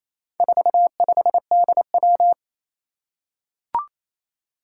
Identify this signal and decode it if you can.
Morse code, heard as a keyed tone, about 29 wpm, 700 Hz. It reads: 45DW
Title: Four Challenging Letter-Number Combinations 28wpm